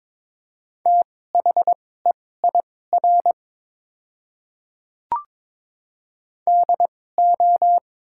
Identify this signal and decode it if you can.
Morse code, heard as a keyed tone, about 22 wpm, spaced as for 16 wpm, 700 Hz. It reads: THEIR DO
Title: 2x Character Spacing - Top 300 Words - 22wpm